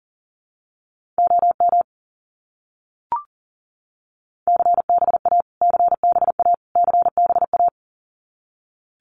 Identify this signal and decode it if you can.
Morse code, heard as a keyed tone, about 40 wpm, 700 Hz. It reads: OM CBA CBA CBA
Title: Abbreviations Repeated 3x - 40wpm